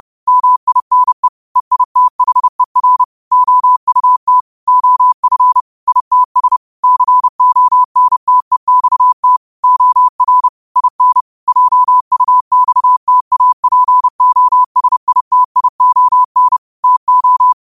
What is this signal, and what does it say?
Morse code, heard as a keyed tone, about 30 wpm, 1000 Hz. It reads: MINE EITHER OUT OF ITS CONTEXT OR IN JUXTAPOSITION TO
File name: qotdfiles / 30wpm